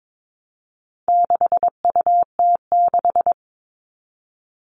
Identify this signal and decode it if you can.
Morse code, heard as a keyed tone, about 22 wpm, 700 Hz. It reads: 6UT6
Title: All Cut Numbers - 4-Digit 22wpm